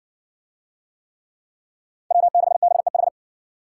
Morse code, heard as a keyed tone, about 60 wpm, 700 Hz. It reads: W6BEH